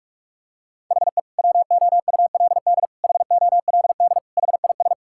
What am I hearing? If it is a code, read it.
Morse code, heard as a keyed tone, about 46 wpm, 700 Hz. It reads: HE WOULD HOLD HIS
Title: Sentences from Top 1000 Words - Review Part 6 - 45wpm